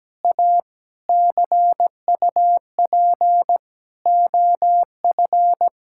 Morse code, heard as a keyed tone, about 17 wpm, 700 Hz. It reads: A CUP OF
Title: Sentences from Top 1600 Words 17wpm